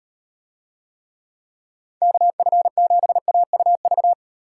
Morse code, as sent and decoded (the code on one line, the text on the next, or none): -.- ..-. --... .- ..- ...-
KF7AUV